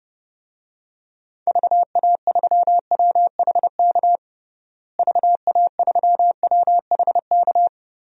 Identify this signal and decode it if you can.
Morse code, heard as a keyed tone, about 30 wpm, 700 Hz. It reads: VA3WHK VA3WHK
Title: DXCC-100 Call Signs - 6 Characters - Repeated 2x - 30wpm